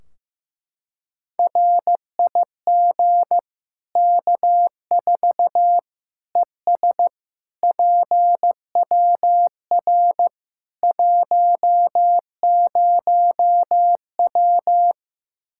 Morse code, heard as a keyed tone, about 15 wpm, 700 Hz. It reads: RIG K4 ES PWR 10W
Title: Practice QSO - Exchange 2 - Info - 15wpm